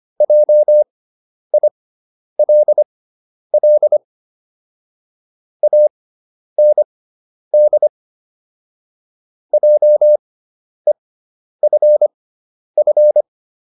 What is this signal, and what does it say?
Morse code, heard as a keyed tone, about 25 wpm, spaced as for 10 wpm, 600 Hz. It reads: JILL AND JEFF